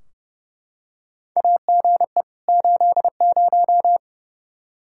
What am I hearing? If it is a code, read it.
Morse code, heard as a keyed tone, about 32 wpm, 700 Hz. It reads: AGE 80
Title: K1USN SST QSO Practice 30wpm